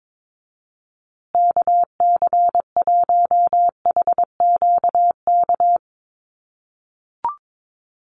Morse code, heard as a keyed tone, about 22 wpm, 700 Hz. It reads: KC1HQK
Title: USA Call Signs Repeated 3x 22wpm